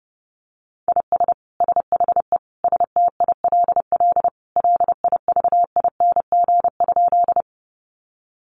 Morse code, heard as a keyed tone, about 28 wpm, 700 Hz. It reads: IS SHE STILL LIVING?
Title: Sentences from Top 1100 Words 30wpm